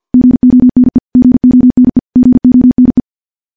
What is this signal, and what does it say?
Morse code, heard as a keyed tone, about 50 wpm, 250 Hz. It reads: GONE GONE GONE
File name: morse / morse-training-50WPM